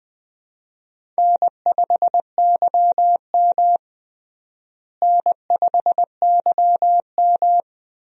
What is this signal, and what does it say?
Morse code, heard as a keyed tone, about 20 wpm, 690 Hz. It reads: N5YM N5YM
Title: USA Call Signs - 4 Characters - Repeated 3x - 20wpm